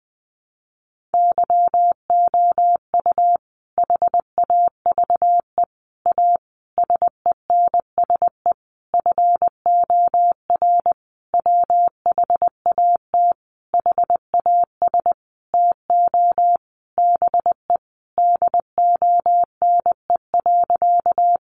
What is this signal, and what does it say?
Morse code, heard as a keyed tone, about 20 wpm, 700 Hz. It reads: YOU HAVE A SENSE FOR WHAT HAS TO BE DONE.
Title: Sentences from Top 700 Words 20wpm